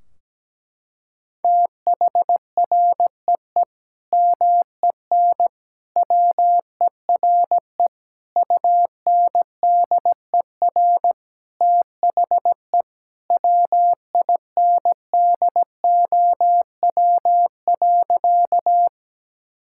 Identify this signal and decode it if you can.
Morse code, heard as a keyed tone, about 17 wpm, 700 Hz. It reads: THREE MEN WERE UNDER THE WINDOW.
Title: Sentences from Top 1600 Words 17wpm